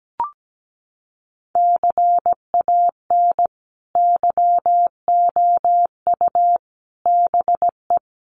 Morse code, heard as a keyed tone, about 17 wpm, 700 Hz. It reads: CAN YOU BE
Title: Sentences from Top 1100 Words 17wpm